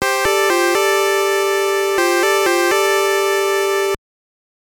There is no code.